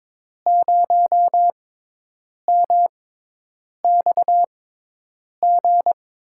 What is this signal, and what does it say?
Morse code, heard as a keyed tone, about 22 wpm, spaced as for 8 wpm, 700 Hz. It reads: 0MXG
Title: USA Call Signs - 5 Characters - 6x Character Spacing - 22wpm